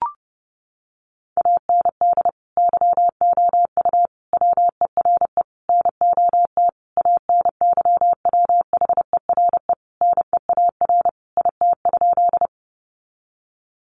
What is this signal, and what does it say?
Morse code, heard as a keyed tone, about 30 wpm, 700 Hz. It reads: AND YOU WERE NOT ANYWHERE NEAR IT?
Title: Sentences from Top 1400 Words 30wpm